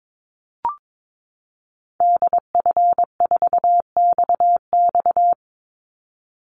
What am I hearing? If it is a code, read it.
Morse code, heard as a keyed tone, about 22 wpm, 700 Hz. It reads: DF4XX
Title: Call Signs Repeated 3x - 22wpm